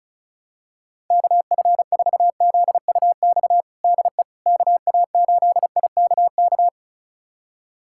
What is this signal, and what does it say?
Morse code, heard as a keyed tone, about 36 wpm, 700 Hz. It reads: KF4ZUX DE KA8IKK
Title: USA Callsign DE Callsign - 6 Characters - 35wpm